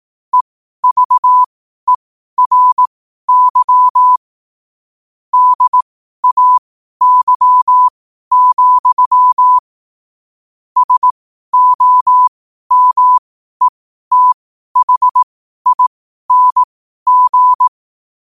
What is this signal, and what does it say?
Morse code, heard as a keyed tone, about 18 wpm, spaced as for 12 wpm, 1000 Hz. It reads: EVERY DAY, SOMETHING
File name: qotdfiles / 12wpm